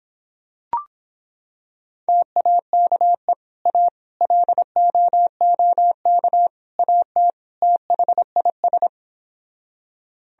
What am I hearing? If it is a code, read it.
Morse code, heard as a keyed tone, about 26 wpm, 700 Hz. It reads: TAKE A LOOK AT THIS